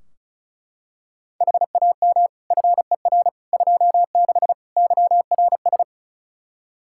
Morse code, heard as a keyed tone, about 36 wpm, 700 Hz. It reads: HAM FER 26 YRS
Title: Practice QSO - Exchange 3 - More Info - 35wpm